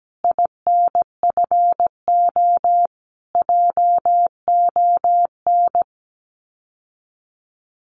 Morse code, heard as a keyed tone, about 17 wpm, 700 Hz. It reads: INFO JON
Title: Common QSO Phrases Encore 17wpm